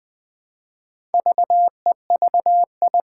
Morse code, heard as a keyed tone, about 21 wpm, 700 Hz. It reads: VEVI
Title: Four Challenging Letter-Number Combinations 20wpm